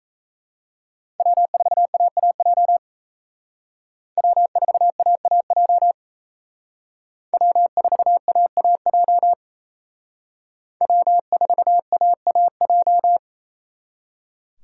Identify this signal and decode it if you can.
Morse code, heard as a keyed tone, about 35 wpm, 700 Hz. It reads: W4AAJ W4AAJ W4AAJ W4AAJ